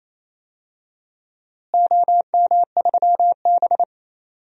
Morse code, heard as a keyed tone, about 28 wpm, 700 Hz. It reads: OM3B
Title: DXCC-100 Call Signs - 4 Characters - Repeated 3x - 28wpm